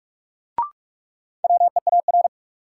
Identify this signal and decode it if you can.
Morse code, heard as a keyed tone, about 41 wpm, 700 Hz. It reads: WEAR
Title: Sentences from Top 1500 Words - New Word Drill Alt - 45wpm